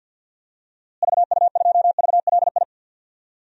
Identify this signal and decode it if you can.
Morse code, heard as a keyed tone, about 51 wpm, 700 Hz. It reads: VU2VLI
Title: DXCC-100 Call Signs - 6 Characters - Repeated 2x - 50wpm